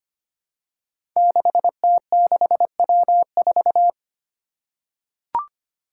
Morse code, heard as a keyed tone, about 25 wpm, 700 Hz. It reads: 6T6W4
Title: Five Challenging Letter-Number Combinations 25wpm